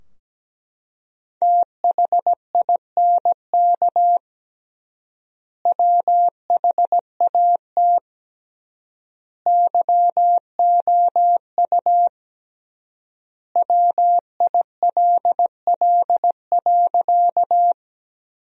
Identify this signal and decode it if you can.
Morse code, heard as a keyed tone, about 17 wpm, 700 Hz. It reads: THINK WHAT YOU WILL.
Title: 3x Word-Spacing - Sentences from Top 200 Words - 17wpm